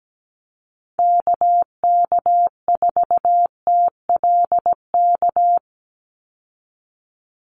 Morse code, heard as a keyed tone, about 17 wpm, 700 Hz. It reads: KK4TLK
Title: USA Call Signs 17wpm